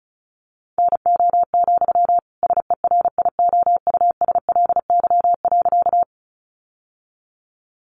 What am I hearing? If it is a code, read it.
Morse code, heard as a keyed tone, about 34 wpm, 700 Hz. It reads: NO, SERIOUSLY.
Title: Sentences from Top 1200 Words 35wpm